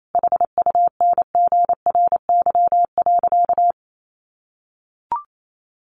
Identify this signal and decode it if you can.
Morse code, heard as a keyed tone, about 28 wpm, 700 Hz. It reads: HUNGRY.
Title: Sentences from Top 1400 Words 28wpm